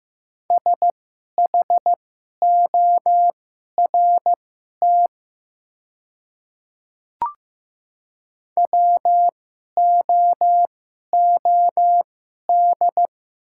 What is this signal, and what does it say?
Morse code, heard as a keyed tone, about 15 wpm, spaced as for 11 wpm, 690 Hz. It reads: SHORT WOOD
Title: 2x Character Spacing - Top 500 Words - 15wpm